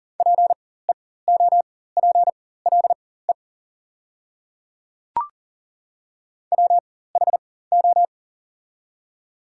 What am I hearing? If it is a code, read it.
Morse code, heard as a keyed tone, about 40 wpm, spaced as for 19 wpm, 700 Hz. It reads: PEOPLE WHO
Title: 4x Character Spacing - Top 100 Words - 40wpm